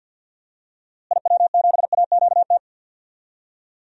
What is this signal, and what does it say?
Morse code, heard as a keyed tone, about 51 wpm, 700 Hz. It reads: IW7AQT
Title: DXCC-100 Call Signs - 6 Characters - Repeated 2x - 50wpm